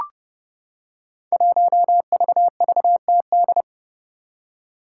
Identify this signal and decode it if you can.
Morse code, heard as a keyed tone, about 30 wpm, 700 Hz. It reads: 1VVTD
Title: Five Challenging Letter-Number Combinations 30wpm